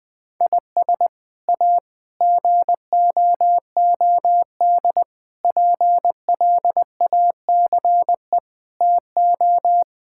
Morse code, heard as a keyed tone, about 20 wpm, 700 Hz. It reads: IS A GOOD PLACE TO
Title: Sets of 6 Words - 20wpm